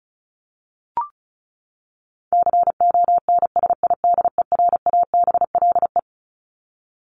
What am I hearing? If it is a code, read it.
Morse code, heard as a keyed tone, about 34 wpm, 700 Hz. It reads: CONSIDERABLE
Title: LY Suffix - Top 100 Words with Base Words 35wpm